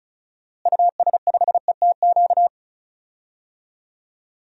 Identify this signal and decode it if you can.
Morse code, heard as a keyed tone, about 36 wpm, 700 Hz. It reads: US5ETQ
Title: DXCC-100 Call Signs - 6 Characters - Repeated 3x - 35wpm